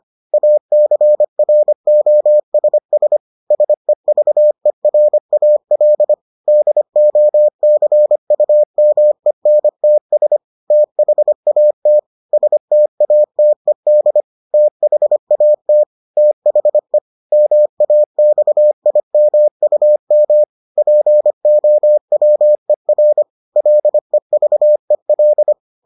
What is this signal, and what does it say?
Morse code, heard as a keyed tone, about 25 wpm, 600 Hz. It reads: ACROSS SEVERAL DOCUMENTS THAT STATED THAT THE MAXIMUM POWER LEVEL